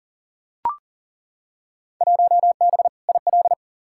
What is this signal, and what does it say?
Morse code, heard as a keyed tone, about 40 wpm, 700 Hz. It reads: 1B IL
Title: Field Day Exchanges - 40wpm